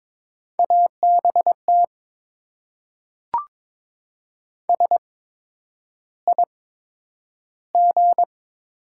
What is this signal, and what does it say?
Morse code, heard as a keyed tone, about 22 wpm, spaced as for 9 wpm, 700 Hz. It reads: ABT SIG
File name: Top 90 QSO Elements - 8x Character Spacing - 22wpm